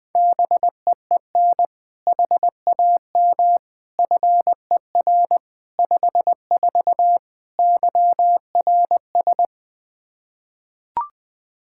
Morse code, heard as a keyed tone, about 21 wpm, 700 Hz. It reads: BEEN HAM FER 54 YRS
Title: Practice QSO - Exchange 3 - More Info - 20wpm